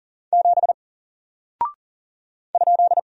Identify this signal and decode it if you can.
Morse code, heard as a keyed tone, about 40 wpm, 700 Hz. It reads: 7 ?
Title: Single Character (Letter:Number:Punctuation:Prosign) Alt 40wpm